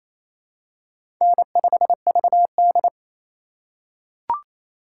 Morse code, heard as a keyed tone, about 28 wpm, 700 Hz. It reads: N5VD
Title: USA Call Signs - 4x Character Spacing - 28wpm